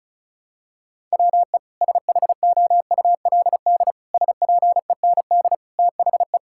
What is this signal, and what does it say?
Morse code, heard as a keyed tone, about 36 wpm, 700 Hz. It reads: WE SHOULD SPEND THE
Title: Sentences from Top 1000 Words - Review Part 5 - 35wpm